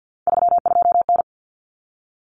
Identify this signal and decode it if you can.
Morse code, heard as a keyed tone, about 50 wpm, 700 Hz. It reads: 32N
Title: Common Cut Numbers - 3-Digit 50wpm